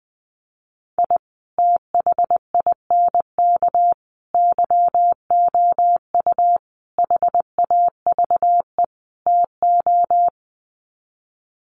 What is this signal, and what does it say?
Morse code, heard as a keyed tone, about 20 wpm, 700 Hz. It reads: I THINK YOU HAVE TO
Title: Sets of 5 Words 20wpm